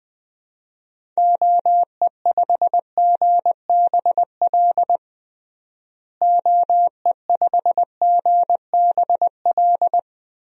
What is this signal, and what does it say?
Morse code, heard as a keyed tone, about 21 wpm, 700 Hz. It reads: OE5GBL OE5GBL
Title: DXCC-100 Call Signs - 6 Characters - Repeated 2x - 20wpm